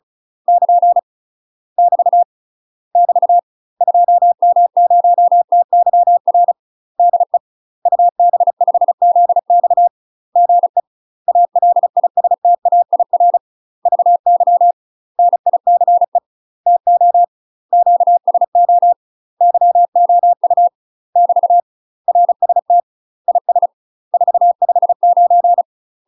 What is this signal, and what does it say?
Morse code, heard as a keyed tone, about 34 wpm, 700 Hz. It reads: <KN> <BT> <BT> 2M0TYR DE UB5ZX GE ALISTAIR VY NICE TO QSO YOU <BT> RST IS 459
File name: UK MASTER QSOs TRAINING / 35WPM / UK MASTER QSOs 0039